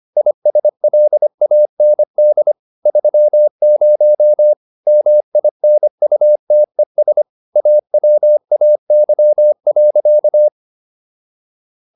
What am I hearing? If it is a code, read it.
Morse code, heard as a keyed tone, about 25 wpm, 600 Hz. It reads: ISLAND 30 MINUTES AWAY.